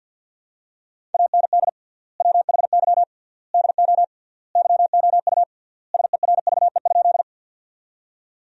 Morse code, heard as a keyed tone, about 50 wpm, 700 Hz. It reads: AND WHY DO YOU SERVE?
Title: 3x Word-Spacing - Sentences from Top 400 Words - 50wpm